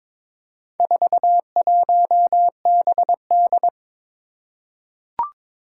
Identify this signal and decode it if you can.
Morse code, heard as a keyed tone, about 22 wpm, 700 Hz. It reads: 41BD
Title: Four Challenging Letter-Number Combinations 22wpm